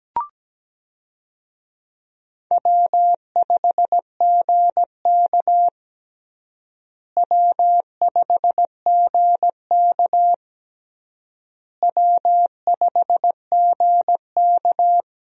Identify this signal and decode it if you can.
Morse code, heard as a keyed tone, about 17 wpm, 700 Hz. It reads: W5GK W5GK W5GK
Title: USA Call Signs - 4 Characters - Repeated 3x - 17wpm